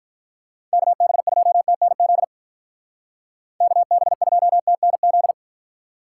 Morse code, heard as a keyed tone, about 43 wpm, 700 Hz. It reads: KB2TNZ KB2TNZ
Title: Speed Racing - USA 2x3 Call Signs 35wpm